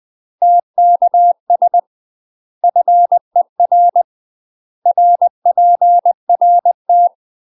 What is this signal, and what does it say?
Morse code, heard as a keyed tone, about 20 wpm, 700 Hz. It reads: TKS FER RPRT
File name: UK MASTER QSOs TRAINING / 20WPM / UK MASTER QSOs 0057